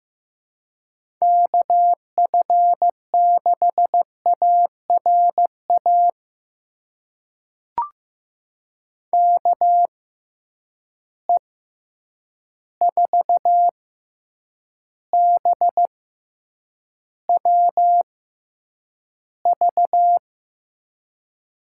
Morse code, heard as a keyed tone, about 15 wpm, spaced as for 8 wpm, 700 Hz. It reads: KF6ARA KE4BWV
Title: USA Call Signs - 6 Characters - 6x Character Spacing - 15wpm